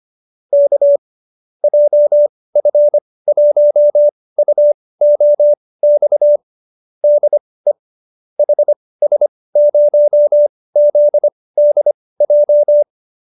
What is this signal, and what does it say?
Morse code, heard as a keyed tone, about 25 wpm, spaced as for 18 wpm, 590 Hz. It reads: K JF1UOX DE HS0ZDJ